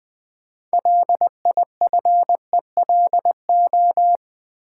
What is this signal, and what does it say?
Morse code, heard as a keyed tone, about 20 wpm, 700 Hz. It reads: LIFELO
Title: Build A Word - 2 Letters At A Time - 8-Letter Words - 20wpm